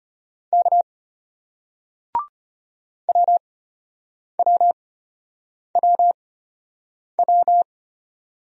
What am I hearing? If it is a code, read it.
Morse code, heard as a keyed tone, about 33 wpm, spaced as for 8 wpm, 700 Hz. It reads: K WWWW